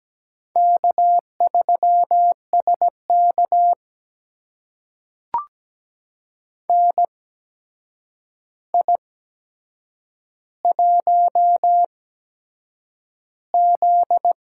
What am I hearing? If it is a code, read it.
Morse code, heard as a keyed tone, about 17 wpm, spaced as for 7 wpm, 700 Hz. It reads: K3SK NI1Z
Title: USA Call Signs - 4 Characters - 8x Character Spacing - 17wpm